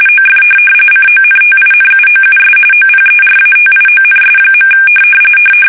RTTY, about 45.45 baud. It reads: OF CONTINUOUS OPERATIONS.
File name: fsk850 / USS-Massachusetts